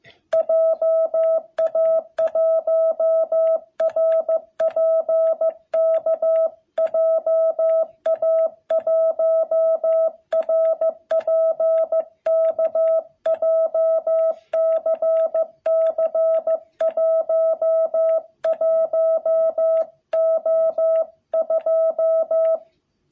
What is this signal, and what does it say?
Morse code, heard as a keyed tone, about 15 wpm, 660 Hz. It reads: JA1RPK JA1RPK JCC11O2
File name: jh8cbh / 2021cw